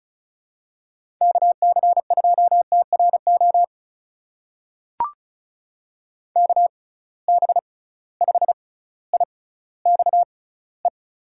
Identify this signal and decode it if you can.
Morse code, heard as a keyed tone, about 36 wpm, spaced as for 18 wpm, 700 Hz. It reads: KC2TRO KB5IXE
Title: USA Call Signs - 6 Characters - 6x Character Spacing - 35wpm